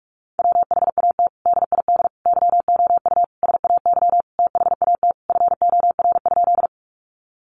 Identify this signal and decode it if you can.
Morse code, heard as a keyed tone, about 44 wpm, 700 Hz. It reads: WHAT DID YOU SAY THAT FOR?
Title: Sentences from Top 200 Words 45wpm